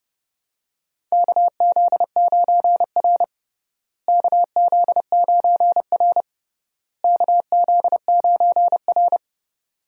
Morse code, heard as a keyed tone, about 30 wpm, 700 Hz. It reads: KZ9R KZ9R KZ9R
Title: USA Call Signs - 4 Characters - Repeated 3x - 30wpm